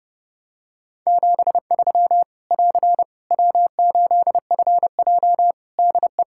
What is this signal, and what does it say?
Morse code, heard as a keyed tone, about 32 wpm, 700 Hz. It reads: 73 <AR> W8FJ DE
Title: Practice QSO - Exchange 4 - Ending - 30wpm